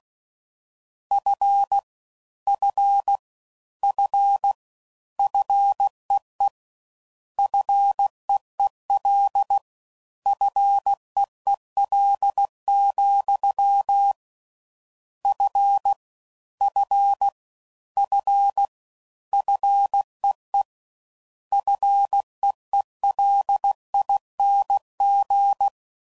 Morse code, heard as a keyed tone, about 16 wpm, 800 Hz. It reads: F F F FEE FEEL FEEL, F F F FEE FEELING